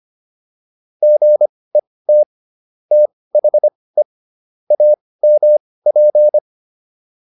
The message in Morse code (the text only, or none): GET THE AMP